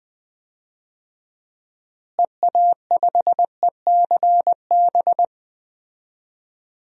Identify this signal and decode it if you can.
Morse code, heard as a keyed tone, about 20 wpm, 700 Hz. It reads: EA5ECB
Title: DXCC-100 Call Signs - 6 Characters - Repeated 3x - 20wpm